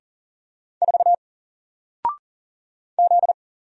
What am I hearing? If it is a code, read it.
Morse code, heard as a keyed tone, about 40 wpm, 700 Hz. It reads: 4 Z
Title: Single Letter-Number Alt 40wpm